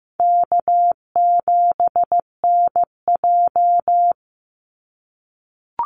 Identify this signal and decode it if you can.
Morse code, heard as a keyed tone, about 15 wpm, 700 Hz. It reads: K7NJ
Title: Easy Call Signs - 15wpm